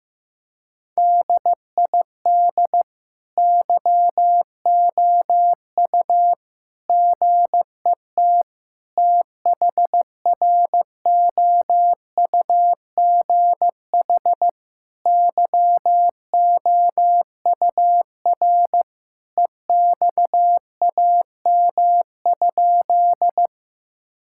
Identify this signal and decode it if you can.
Morse code, heard as a keyed tone, about 15 wpm, 700 Hz. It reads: DID YOU GET THROUGH YOUR EXAM?
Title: Sentences from Top 1000 Words - Review Part 5 - 15wpm